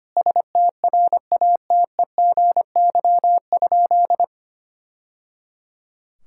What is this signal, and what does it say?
Morse code, heard as a keyed tone, about 25 wpm, 700 Hz. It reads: STRATEGY?
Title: Sentences from Top 1000 Words - Review Part 5 - 25wpm